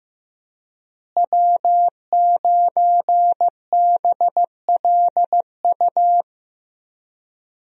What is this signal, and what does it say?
Morse code, heard as a keyed tone, about 15 wpm, 700 Hz. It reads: W9BLU
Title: USA Call Signs - 5 Characters - Repeated 2x - 15wpm